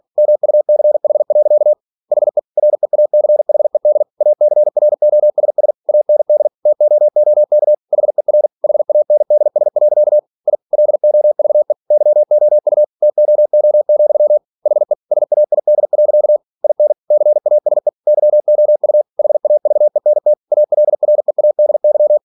Morse code, heard as a keyed tone, about 46 wpm, 600 Hz. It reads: MUCH. HE REACHED ACROSS AND TOOK HER HANDS. I LOVE YOU TOO, HE SAID. IN CASE YOU HAVENT ALREADY
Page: Never 47wpm